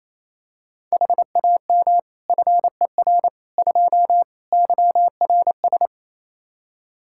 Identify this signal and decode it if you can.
Morse code, heard as a keyed tone, about 27 wpm, 700 Hz. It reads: HAM FER 2 YRS
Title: Practice QSO - Exchange 3 - More Info - 28wpm